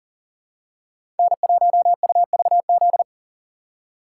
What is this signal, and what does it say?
Morse code, heard as a keyed tone, about 40 wpm, 700 Hz. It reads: N1UVZ
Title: USA Call Signs - 5 Characters - 3x Character Spacing - 40wpm